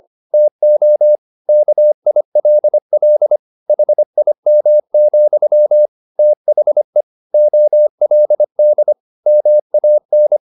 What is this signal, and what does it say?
Morse code, heard as a keyed tone, about 25 wpm, 600 Hz. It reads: TO KILL HIM, THE OLD MAN